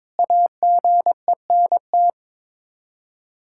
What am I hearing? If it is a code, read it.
Morse code, heard as a keyed tone, about 22 wpm, 700 Hz. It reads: AGENT
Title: K1USN SST QSO Practice 22wpm